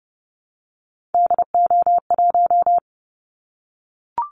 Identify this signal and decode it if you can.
Morse code, heard as a keyed tone, about 30 wpm, 700 Hz. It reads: DO1
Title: Callsign Prefixes for Top 50 DXCC - 30wpm